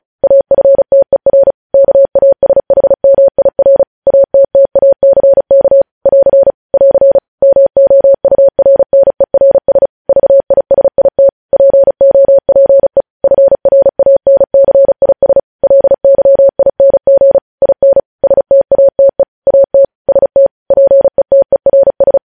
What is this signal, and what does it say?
Morse code, heard as a keyed tone, about 36 wpm, 550 Hz. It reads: AFTER KASHMIR ATTACK <AR> <AR> MOURNERS VISIT POPE FRANCIS LYING IN STATE AT ST PETERS
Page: news headlines in morse code at 35 wpm